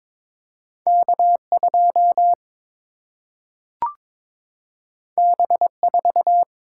Morse code, heard as a keyed tone, about 22 wpm, 700 Hz. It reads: K 2 B 4
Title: Single-Letter Single-Number 22wpm